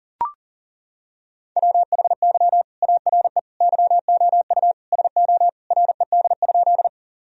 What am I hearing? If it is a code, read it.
Morse code, heard as a keyed tone, about 37 wpm, 700 Hz. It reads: WHY ARE YOU SO RED?
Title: Sentences from Top 1000 Words - Review Part 2 - 40wpm